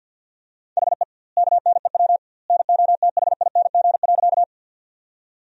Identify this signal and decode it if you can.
Morse code, heard as a keyed tone, about 49 wpm, 700 Hz. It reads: HE KNEW NOTHING.
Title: 2x Word-Spacing - Sentences from Top 500 Words - 50wpm